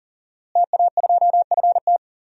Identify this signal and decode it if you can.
Morse code, heard as a keyed tone, about 38 wpm, 700 Hz. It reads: TA2FT
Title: DXCC-100 Call Signs - 5 Characters - Repeated 2x - 40wpm